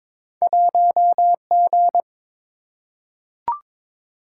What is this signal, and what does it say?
Morse code, heard as a keyed tone, about 22 wpm, 700 Hz. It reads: 1G
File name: practice / 22wpm / Two Letter-Number Combinations - 22wpm